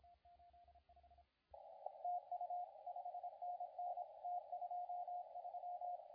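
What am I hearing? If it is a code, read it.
Morse code, heard as a keyed tone, about 26 wpm, 710 Hz, with radio noise. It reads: TU 5NN T23
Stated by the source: CW contest audio recorded at LA4C during the SAC CW contest.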